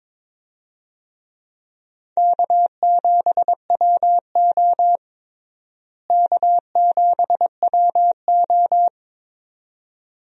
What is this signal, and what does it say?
Morse code, heard as a keyed tone, about 22 wpm, 700 Hz. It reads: K7WO K7WO
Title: USA Call Signs - 4 Characters - Repeated 2x - 22wpm